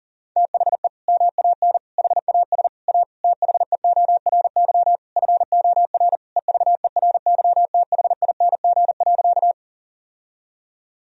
Morse code, heard as a keyed tone, about 40 wpm, 700 Hz. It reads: THE MAN HAS A THEORY FOR EVERYTHING.
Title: Sentences from Top 1000 Words - Review Part 3 - 40wpm